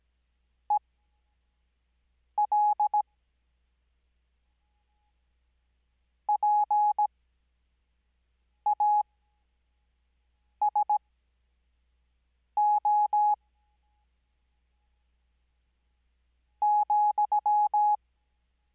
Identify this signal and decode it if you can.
Morse code, heard as a keyed tone, about 17 wpm, spaced as for 5 wpm, 840 Hz. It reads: EL PASO ,
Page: Farnsworth Spacing: 15WPM Character speed Spaced at 5WPM.